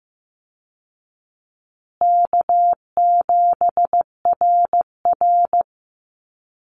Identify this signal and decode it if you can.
Morse code, heard as a keyed tone, about 15 wpm, 700 Hz. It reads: K7RR
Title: Easy Call Signs Repeated 2x - 15wpm